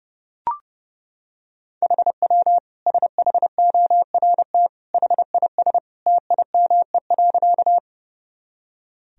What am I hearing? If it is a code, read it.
Morse code, heard as a keyed tone, about 32 wpm, 700 Hz. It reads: HW SHORT HIS TIME.
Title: Sentences from Top 400 Words and Abbreviations 30wpm